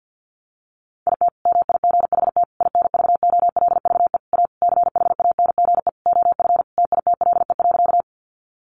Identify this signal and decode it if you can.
Morse code, heard as a keyed tone, about 49 wpm, 700 Hz. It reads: IT MIGHT INVOLVE A CHANGE OF TITLE.
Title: Sentences from Top 1200 Words 50wpm